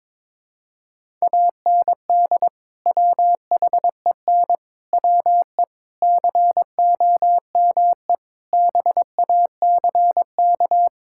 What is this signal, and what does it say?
Morse code, heard as a keyed tone, about 22 wpm, 700 Hz. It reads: AND WHEN WE COME BACK
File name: practice / 22wpm / Build A Phrase - 5-Word Phrases - 22wpm